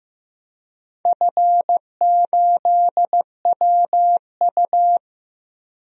Morse code, heard as a keyed tone, about 15 wpm, 700 Hz. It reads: F8WU
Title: DXCC-100 Call Signs - 4 Characters - Repeated 3x - 15wpm